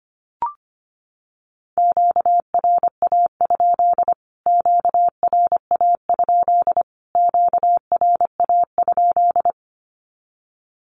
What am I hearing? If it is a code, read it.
Morse code, heard as a keyed tone, about 25 wpm, 700 Hz. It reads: QRA? QRA? QRA?
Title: Q-Codes Repeated 3x 25wpm